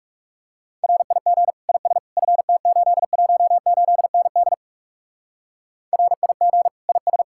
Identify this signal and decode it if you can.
Morse code, heard as a keyed tone, about 36 wpm, 700 Hz. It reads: RIG IS FT818ND RIG IS
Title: Speed-Racing Common QSO Phrases Encore - 30wpm